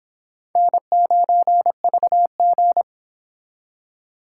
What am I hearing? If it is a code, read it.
Morse code, heard as a keyed tone, about 31 wpm, 700 Hz. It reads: N9VG